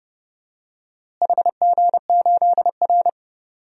Morse code, heard as a keyed tone, about 30 wpm, 700 Hz. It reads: HG8R
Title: DXCC-100 Call Signs - 4 Characters - Repeated 2x - 30wpm